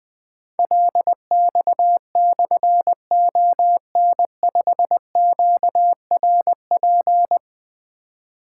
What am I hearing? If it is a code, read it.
Morse code, heard as a keyed tone, about 20 wpm, 700 Hz. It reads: LX/ON5QRP
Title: Difficult Call Signs 20wpm